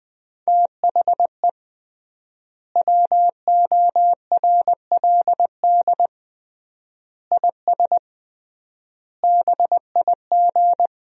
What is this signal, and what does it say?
Morse code, heard as a keyed tone, about 21 wpm, 700 Hz. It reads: THE WORLD IS BIG
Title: 3x Word-Spacing - Sentences from Top 400 Words - 20wpm